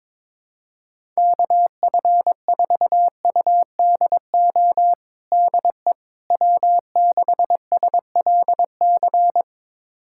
Morse code, heard as a keyed tone, about 22 wpm, 700 Hz. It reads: KF4UDO DE W6SLC
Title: USA Callsign DE Callsign 22wpm